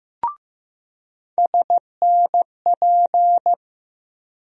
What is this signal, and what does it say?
Morse code, heard as a keyed tone, about 15 wpm, 700 Hz. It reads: SNP
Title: USA Call Sign Postfixes - 3 Letters 15wpm